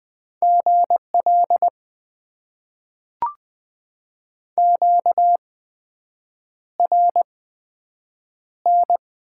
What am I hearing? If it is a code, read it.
Morse code, heard as a keyed tone, about 20 wpm, spaced as for 7 wpm, 700 Hz. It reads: GL QRN
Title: Top 90 QSO Elements - 8x Character Spacing - 20wpm